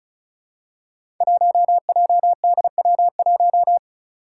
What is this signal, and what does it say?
Morse code, heard as a keyed tone, about 34 wpm, 700 Hz. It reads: 1JDW1
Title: Five Challenging Letter-Number Combinations 35wpm